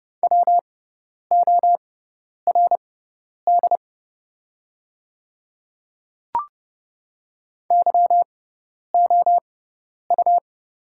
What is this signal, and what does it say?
Morse code, heard as a keyed tone, about 30 wpm, spaced as for 10 wpm, 700 Hz. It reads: WORD YOU
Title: 6x Character Spacing - Top 100 Words - 30wpm